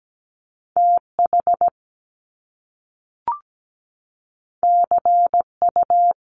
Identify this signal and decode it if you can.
Morse code, heard as a keyed tone, about 17 wpm, 700 Hz. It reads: TH CU
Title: Top Two Letter Combinations in Words 17wpm